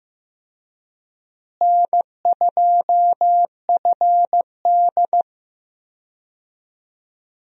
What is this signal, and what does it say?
Morse code, heard as a keyed tone, about 15 wpm, 700 Hz. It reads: N2FD
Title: USA Call Signs - 4 Characters - Repeated 3x - 15wpm